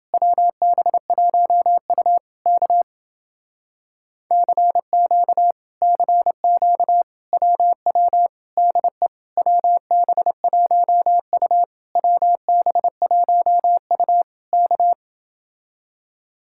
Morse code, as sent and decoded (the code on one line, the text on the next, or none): .-- -... .---- ..- / -.- / -.-. --.- / -.-. --.- / .-- .-- / -.. / .-- -... .---- ..- / .-- -... .---- ..- / -.-
WB1U K CQ CQ WW DE WB1U WB1U K